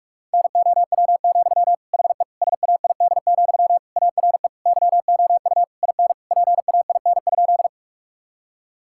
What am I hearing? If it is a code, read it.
Morse code, heard as a keyed tone, about 46 wpm, 700 Hz. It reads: NOW, HE SAID, ARE YOU IN PAIN?
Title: Sentences from Top 1600 Words 45wpm